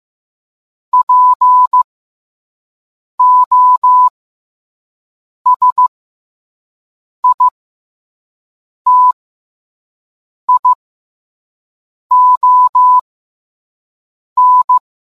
Morse code, heard as a keyed tone, about 15 wpm, spaced as for 5 wpm, 1000 Hz. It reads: POSITION